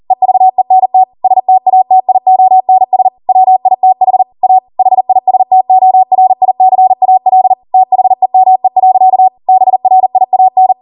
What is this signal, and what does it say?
Morse code, heard as a keyed tone, about 40 wpm, 750 Hz. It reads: EVENT STATIONS WITH A HISTORICAL THEME. BRIAN